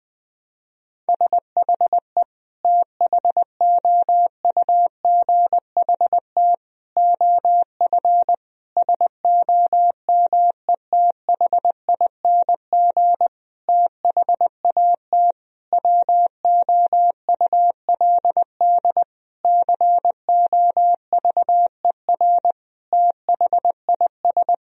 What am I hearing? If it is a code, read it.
Morse code, heard as a keyed tone, about 20 wpm, 700 Hz. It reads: SHE THOUGHT OF SOMETHING THAT WOULD COVER THIS
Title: Sentences from Top 1000 Words - Review Part 1 - 20wpm